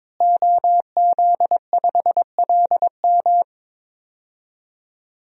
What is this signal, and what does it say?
Morse code, heard as a keyed tone, about 22 wpm, 700 Hz. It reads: OZ5LM
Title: DXCC-100 Call Signs - 5 Characters - Repeated 3x - 22wpm